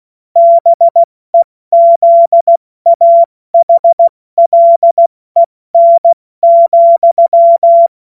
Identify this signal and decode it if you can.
Morse code, heard as a keyed tone, about 16 wpm, spaced as for 14 wpm, 680 Hz. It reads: BEZAHLEN,
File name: Morsekurs / Witze / Witz 19. Oktober 2021, 70 F80